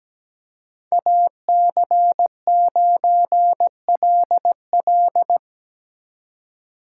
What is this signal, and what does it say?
Morse code, heard as a keyed tone, about 20 wpm, 700 Hz. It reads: AC9LL
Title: Speed Racing - USA 2x2 Call Signs 17wpm